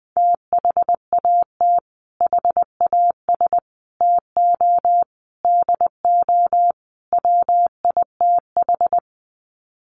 Morse code, heard as a keyed tone, about 20 wpm, 700 Hz. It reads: THAT HAS TO DO WITH
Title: Sets of 5 Words Encore2 20wpm